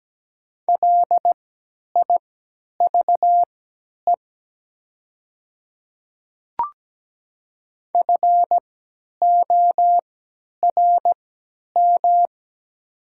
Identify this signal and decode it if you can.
Morse code, heard as a keyed tone, about 17 wpm, spaced as for 10 wpm, 700 Hz. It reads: LIVE FORM
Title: 3x Character Spacing - Top 500 Words - 17wpm